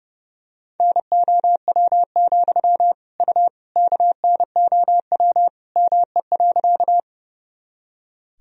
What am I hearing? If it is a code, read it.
Morse code, heard as a keyed tone, about 30 wpm, 700 Hz. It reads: NOW, U KNOW ME.
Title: Sentences from Top 200 Words and Abbreviations 30wpm